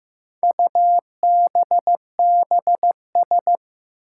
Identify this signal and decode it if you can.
Morse code, heard as a keyed tone, about 16 wpm, 700 Hz. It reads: UBBS
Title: Four Challenging Letter-Number Combinations 15wpm